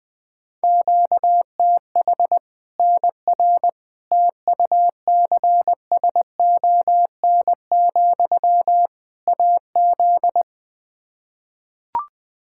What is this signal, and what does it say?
Morse code, heard as a keyed tone, about 21 wpm, 700 Hz. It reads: QTH NR TUCSON, AZ
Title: K1USN SST QSO Practice 20wpm